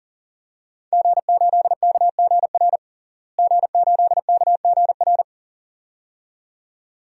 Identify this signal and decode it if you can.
Morse code, heard as a keyed tone, about 40 wpm, 700 Hz. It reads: G8KGR G8KGR
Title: DXCC-100 Call Signs - 5 Characters - Repeated 2x - 40wpm